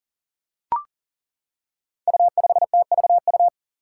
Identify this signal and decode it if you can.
Morse code, heard as a keyed tone, about 40 wpm, 700 Hz. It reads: U5TVU
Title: Five Challenging Letter-Number Combinations 40wpm